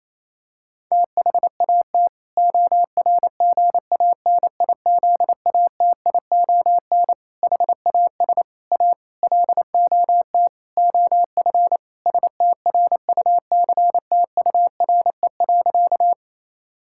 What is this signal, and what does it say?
Morse code, heard as a keyed tone, about 28 wpm, 700 Hz. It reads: THAT ORGANIZATION HAS A LOT OF STRUCTURE.
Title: Sentences from Top 1000 Words - Review Part 6 - 28wpm